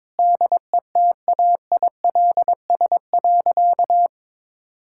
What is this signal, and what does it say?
Morse code, heard as a keyed tone, about 22 wpm, 700 Hz. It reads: DETAILS.
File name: Sentences with 9-Word Phrases - 22wpm